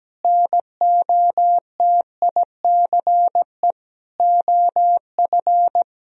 Morse code, heard as a keyed tone, about 17 wpm, 700 Hz. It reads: NOTICE OF
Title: Sentences from Top 1000 Words - Review Part 4 - 17wpm